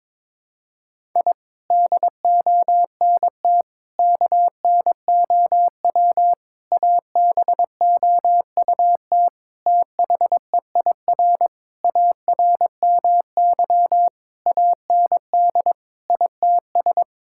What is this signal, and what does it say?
Morse code, heard as a keyed tone, about 22 wpm, 700 Hz. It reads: I DONT KNOW ABOUT THEIR ARMY AND ITS
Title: Sentences from Top 1000 Words - Review Part 5 - 22wpm